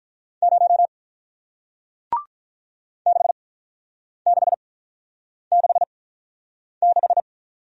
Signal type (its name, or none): Morse code